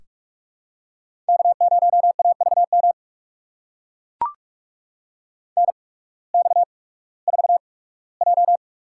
Morse code, heard as a keyed tone, about 42 wpm, spaced as for 21 wpm, 700 Hz. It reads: K0AVM NX4J
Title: USA Call Signs - 8x Character Spacing - 45wpm